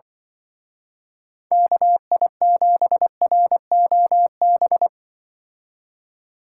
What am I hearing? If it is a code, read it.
Morse code, heard as a keyed tone, about 24 wpm, 700 Hz. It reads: KI7ROB
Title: Speed Racing - USA 2x3 Call Signs 20wpm